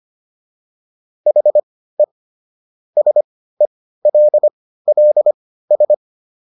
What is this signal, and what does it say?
Morse code, heard as a keyed tone, about 25 wpm, spaced as for 15 wpm, 600 Hz. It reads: HE SELLS